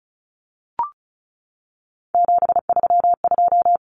Morse code, heard as a keyed tone, about 34 wpm, 700 Hz. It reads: 732
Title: Common Cut Numbers - 3-Digit 35wpm